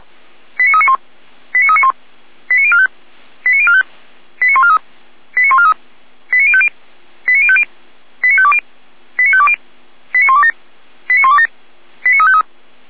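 Five-tone selective call (ZVEI), heard as quick runs of five tones, tones of about 70 ms. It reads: 89391 89391 89045 89045 89143 89143 89050 89050 89520 89520 89127 89127 89353
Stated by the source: ZVEI five-tone call received by SDR.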